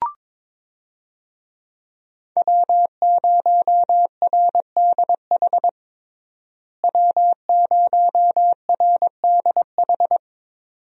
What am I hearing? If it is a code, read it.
Morse code, heard as a keyed tone, about 22 wpm, 700 Hz. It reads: W0RDH W0RDH
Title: USA Call Signs - 5 Characters - Repeated 2x - 22wpm